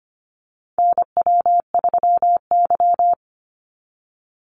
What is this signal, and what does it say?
Morse code, heard as a keyed tone, about 25 wpm, 700 Hz. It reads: NW3Y
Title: Call Signs Repeated 2x - 25wpm